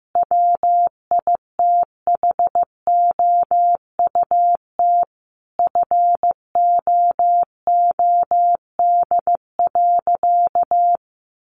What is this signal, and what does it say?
Morse code, heard as a keyed tone, about 15 wpm, 700 Hz. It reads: WITHOUT FOOD.
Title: Sentences from Top 1500 Words 15wpm